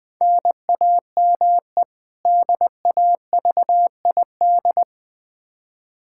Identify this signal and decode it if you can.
Morse code, heard as a keyed tone, about 21 wpm, 700 Hz. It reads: NAME DAVID
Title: K1USN SST QSO Practice 20wpm